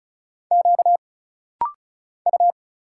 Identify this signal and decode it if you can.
Morse code, heard as a keyed tone, about 34 wpm, 700 Hz. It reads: Q U
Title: Single Letter-Number Alt - Rapid-Fire 35wpm